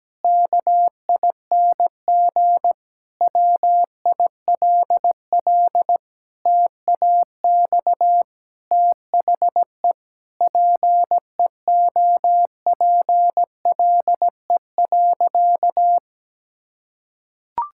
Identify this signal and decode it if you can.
Morse code, heard as a keyed tone, about 17 wpm, 700 Hz. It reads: KING WILL TAX THE PEOPLE.
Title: Sentences from Top 1000 Words - Review Part 6 - 17wpm